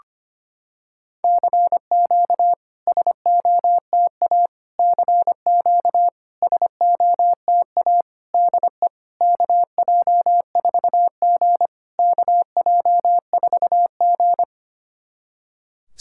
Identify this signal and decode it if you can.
Morse code, heard as a keyed tone, about 25 wpm, 700 Hz. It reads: CQ SOTA CQ SOTA DE KJ4G KJ4G
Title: SOTA QSO Practice 25wpm